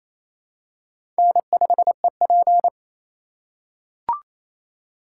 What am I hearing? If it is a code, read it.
Morse code, heard as a keyed tone, about 28 wpm, 700 Hz. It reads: N5EP
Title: USA Call Signs - 4 Characters - 2x Character Spacing - 28wpm